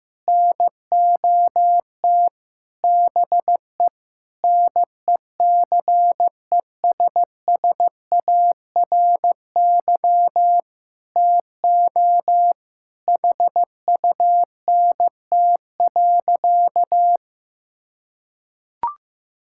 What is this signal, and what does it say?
Morse code, heard as a keyed tone, about 16 wpm, 700 Hz. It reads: NOT BE NECESSARY TO HUNT.
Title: Sentences from Top 1600 Words 15wpm